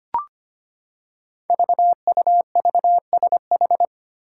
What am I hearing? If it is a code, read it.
Morse code, heard as a keyed tone, about 25 wpm, 700 Hz. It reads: VUVSH
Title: Five Challenging Letter-Number Combinations 25wpm